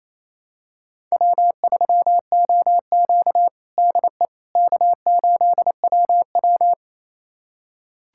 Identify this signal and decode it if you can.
Morse code, heard as a keyed tone, about 28 wpm, 700 Hz. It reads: W3OQ DE K8WW
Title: USA Callsign DE Callsign - 4 Characters - 28wpm